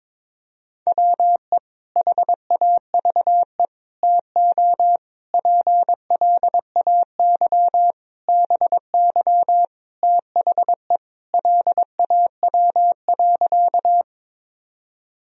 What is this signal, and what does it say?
Morse code, heard as a keyed tone, about 21 wpm, 700 Hz. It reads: WE HAVE TO PLAY BY THE LAW.
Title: Sentences from Top 1000 Words - Review Part 3 - 22wpm